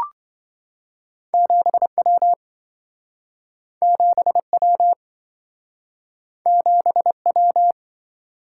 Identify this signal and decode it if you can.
Morse code, heard as a keyed tone, about 28 wpm, 700 Hz. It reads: 7W 7W 7W